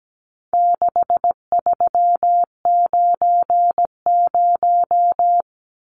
Morse code, heard as a keyed tone, about 17 wpm, 700 Hz. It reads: 6390
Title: Numbers - 4-Digit 17wpm